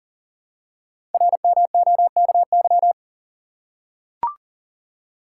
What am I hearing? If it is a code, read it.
Morse code, heard as a keyed tone, about 44 wpm, 700 Hz. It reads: RMOKY
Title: Code Groups - 5 Letters - 40wpm